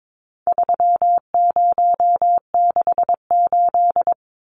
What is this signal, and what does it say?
Morse code, heard as a keyed tone, about 22 wpm, 700 Hz. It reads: 3068
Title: Numbers - 4-Digit 22wpm